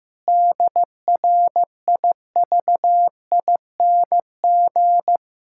Morse code, heard as a keyed tone, about 15 wpm, 700 Hz. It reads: DRIVING
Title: Pangram Challenge v2 - 15wpm